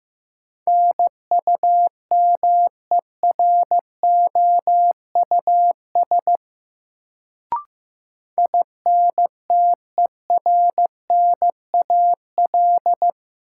Morse code, heard as a keyed tone, about 15 wpm, 700 Hz. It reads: NUMEROUS INTERNAL
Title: Sentences from Top 1400 Words - New Word Drill - 15wpm